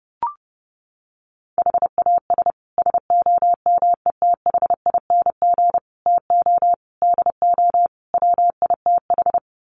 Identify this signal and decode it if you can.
Morse code, heard as a keyed tone, about 30 wpm, 700 Hz. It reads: HAS SOMETHING TO DO WITH
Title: Sets of 5 Words Encore 30wpm